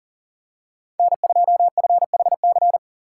Morse code, heard as a keyed tone, about 40 wpm, 700 Hz. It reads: N2FHC
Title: USA Call Signs - 5 Characters - 3x Character Spacing - 40wpm